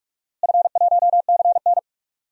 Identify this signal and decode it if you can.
Morse code, heard as a keyed tone, about 47 wpm, 700 Hz. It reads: F1CN
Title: DXCC-100 Call Signs - 4 Characters - Repeated 3x - 45wpm